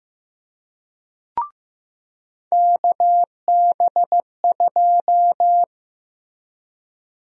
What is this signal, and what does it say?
Morse code, heard as a keyed tone, about 15 wpm, 700 Hz. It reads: KB2
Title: USA Call Sign Prefixes - Repeated 2x - 15wpm